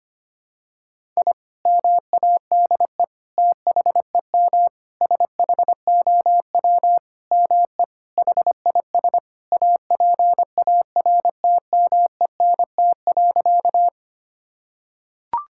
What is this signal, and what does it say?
Morse code, heard as a keyed tone, about 25 wpm, 700 Hz. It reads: I MADE THEM SHOW ME HIS APARTMENT.
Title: Sentences from Top 1000 Words - Review Part 6 - 25wpm